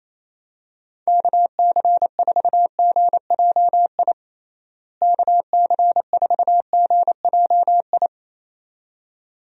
Morse code, heard as a keyed tone, about 27 wpm, 700 Hz. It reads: KC4GJI KC4GJI
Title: USA Call Signs - 6 Characters - Repeated 2x - 28wpm